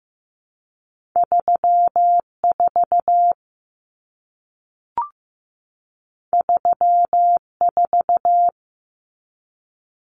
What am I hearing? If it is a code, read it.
Morse code, heard as a keyed tone, about 15 wpm, 700 Hz. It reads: 34 34
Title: Common Cut Numbers - 2-Digit 15wpm